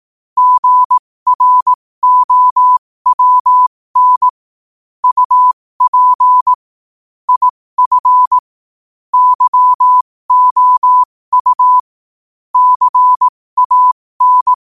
Morse code, heard as a keyed tone, about 18 wpm, spaced as for 15 wpm, 1000 Hz. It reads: GROWN UP IF YOU CAN
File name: qotdfiles / 15wpm